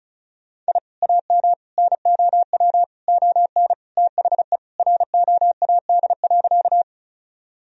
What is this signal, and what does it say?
Morse code, heard as a keyed tone, about 36 wpm, 700 Hz. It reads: I AM NOW ON THE ROAD.
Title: Sentences from Top 1000 Words - Review Part 2 - 35wpm